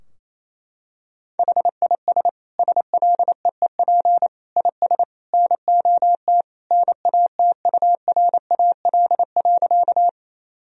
Morse code, heard as a keyed tone, about 28 wpm, 700 Hz. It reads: HIS SLEEP IS NOT NATURAL.
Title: Sentences from Top 1000 Words - Review Part 4 - 28wpm